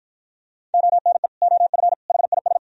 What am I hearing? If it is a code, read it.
Morse code, heard as a keyed tone, about 47 wpm, 700 Hz. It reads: ONE OF HIS